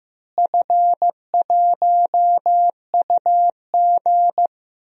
Morse code, heard as a keyed tone, about 15 wpm, 700 Hz. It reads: F1UG
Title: DXCC-100 Call Signs - 4 Characters - Repeated 3x - 15wpm